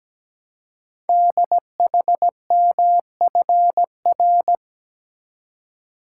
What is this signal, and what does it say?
Morse code, heard as a keyed tone, about 17 wpm, 700 Hz. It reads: DHMFR
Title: Code Groups - 5 Letters - 17wpm